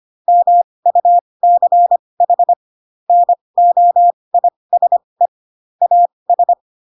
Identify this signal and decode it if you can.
Morse code, heard as a keyed tone, about 25 wpm, spaced as for 20 wpm, 700 Hz. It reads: MUCH NOISE AS